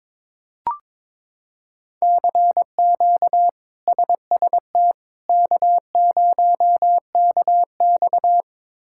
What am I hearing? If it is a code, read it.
Morse code, heard as a keyed tone, about 22 wpm, 700 Hz. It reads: CQ SST K0KX
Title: K1USN SST QSO Practice - 22wpm